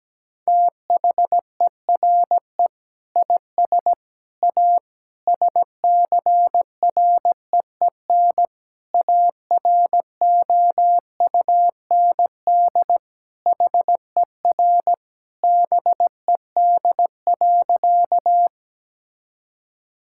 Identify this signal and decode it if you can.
Morse code, heard as a keyed tone, about 17 wpm, 700 Hz. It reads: THERE IS A SCREEN AROUND HER BED.
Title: Sentences from Top 1000 Words - Review Part 6 - 17wpm